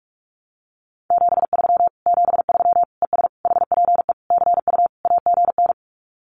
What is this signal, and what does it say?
Morse code, heard as a keyed tone, about 44 wpm, 700 Hz. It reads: 73 73 ES HPE CU AGN
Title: Common QSO Phrases Encore 45wpm